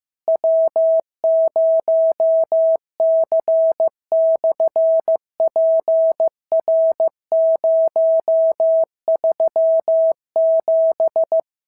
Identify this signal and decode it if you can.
Morse code, heard as a keyed tone, about 15 wpm, 650 Hz. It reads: W0C/PR037
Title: SOTA QSO Practice 15wpm